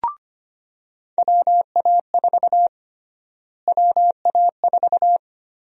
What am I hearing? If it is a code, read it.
Morse code, heard as a keyed tone, about 25 wpm, 700 Hz. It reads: WA4 WA4
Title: USA Call Sign Prefixes - Repeated 2x - 25wpm